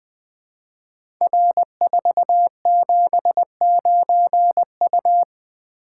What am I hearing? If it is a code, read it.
Morse code, heard as a keyed tone, about 20 wpm, 700 Hz. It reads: R479U
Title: Code Groups - 5 Letter:Numbers - 20wpm